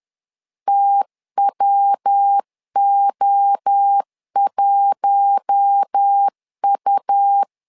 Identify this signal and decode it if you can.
Morse code, heard as a keyed tone, about 10 wpm, 800 Hz. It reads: TWO1U
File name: LEZIONI / 4 Q1 10wpm 2448